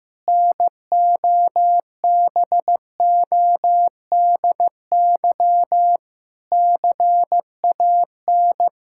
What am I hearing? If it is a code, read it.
Morse code, heard as a keyed tone, about 16 wpm, 700 Hz. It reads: NOBODY CAN
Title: Sentences from Top 1600 Words 15wpm